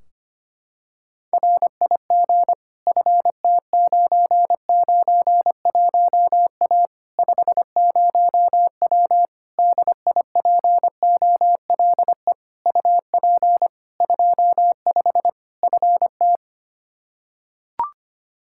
Morse code, heard as a keyed tone, about 25 wpm, 700 Hz. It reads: RIG FT991A 50W DIPOLE UP 25 FT
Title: Practice QSO - Exchange 2 - Info - 25wpm